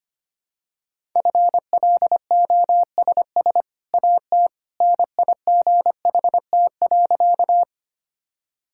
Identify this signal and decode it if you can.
Morse code, heard as a keyed tone, about 25 wpm, 700 Hz. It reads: FLOSS AT NIGHT.
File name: Easy Sentences with Hints - 25wpm